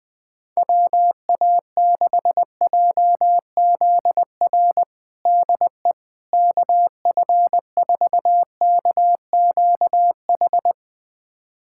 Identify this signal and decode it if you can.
Morse code, heard as a keyed tone, about 20 wpm, 700 Hz. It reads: WA6JZR DE KF4KQH
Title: USA Callsign DE Callsign - 6 Characters - 20wpm